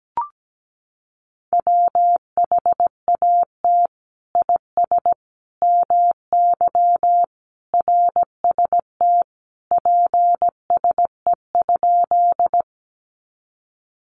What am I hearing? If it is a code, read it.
Morse code, heard as a keyed tone, about 17 wpm, 700 Hz. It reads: WHAT IS MY RST PSE?
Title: Common QSO Phrases 17wpm